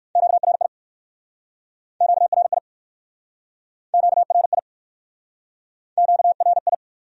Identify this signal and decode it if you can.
Morse code, heard as a keyed tone, about 56 wpm, 700 Hz. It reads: QRI QRI QRI QRI
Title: Speed-Racing Q-Codes - 45wpm